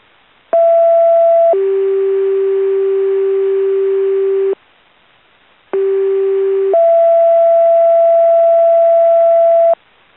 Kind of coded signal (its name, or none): two-tone page